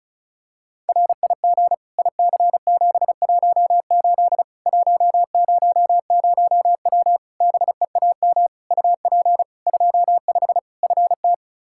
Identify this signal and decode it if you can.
Morse code, heard as a keyed tone, about 34 wpm, 700 Hz. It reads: RIG IC718 100W BEAM UP 25 FT
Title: Practice QSO - Exchange 2 - Info - 35wpm